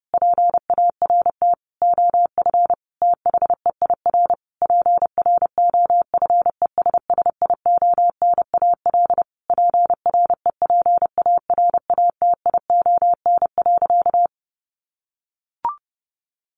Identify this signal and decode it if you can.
Morse code, heard as a keyed tone, about 30 wpm, 700 Hz. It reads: PART OF THEIR PROFESSIONAL PREPARATION.
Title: Sentences from Top 1400 Words 30wpm